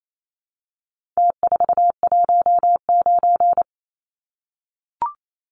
Morse code, heard as a keyed tone, about 28 wpm, 700 Hz. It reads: T419
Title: Numbers - Contest Style - 4-Digit 28wpm